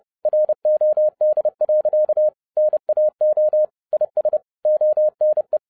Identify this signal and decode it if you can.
Morse code, heard as a keyed tone, about 30 wpm, 610 Hz. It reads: ROD. NAO IS ONE